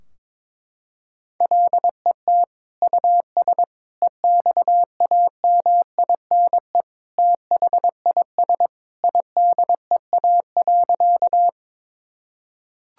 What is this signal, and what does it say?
Morse code, heard as a keyed tone, about 22 wpm, 700 Hz. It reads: LET US EXAMINE THIS IDEA.
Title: Sentences from Top 1600 Words 22wpm